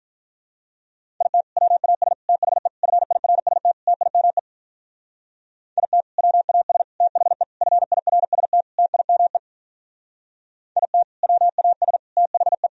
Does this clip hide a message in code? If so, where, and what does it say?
Yes: Morse code from 1.2 to 12.7 s, heard as a keyed tone, about 45 wpm, 700 Hz. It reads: IT WAS THE FIRST TIME IT WAS THE FIRST TIME IT WAS THE